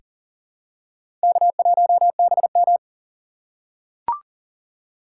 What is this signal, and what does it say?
Morse code, heard as a keyed tone, about 37 wpm, 700 Hz. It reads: K1BM
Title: USA Call Signs - 4 Characters - 6x Character Spacing - 40wpm